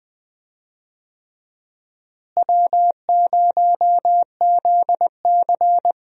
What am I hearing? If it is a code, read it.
Morse code, heard as a keyed tone, about 20 wpm, 700 Hz. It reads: W0ZC
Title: USA Call Signs - 4 Characters - Repeated 3x - 20wpm